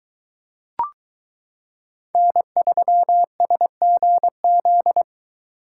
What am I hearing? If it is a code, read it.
Morse code, heard as a keyed tone, about 22 wpm, 700 Hz. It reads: N3SGZ